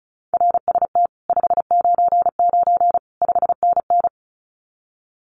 Morse code, heard as a keyed tone, about 34 wpm, 700 Hz. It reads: RST 599 5NN
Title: Common QSO Phrases Encore 35wpm